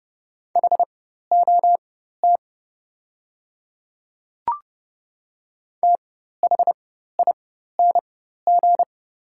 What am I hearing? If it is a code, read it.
Morse code, heard as a keyed tone, about 30 wpm, spaced as for 14 wpm, 700 Hz. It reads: HOT THING
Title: 4x Character Spacing - Top 100 Words - 30wpm